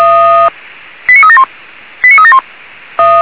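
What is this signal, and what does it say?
Five-tone selective call (ZVEI), heard as quick runs of five tones, tones of about 70 ms. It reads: 89371 89371
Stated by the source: Five-tone ZVEI selective call 89370 received by SDR.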